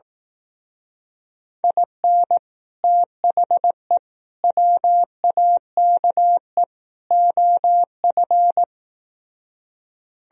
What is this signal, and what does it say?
Morse code, heard as a keyed tone, about 18 wpm, 700 Hz. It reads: IN THE WAKE OF
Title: Speed-Racing - Sets of 4 Words 15wpm